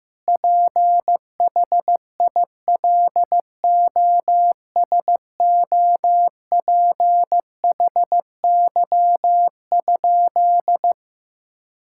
Morse code, heard as a keyed tone, about 15 wpm, 700 Hz. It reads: PHILOSOPHY?
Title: Sentences from Top 1000 Words - Review Part 6 - 15wpm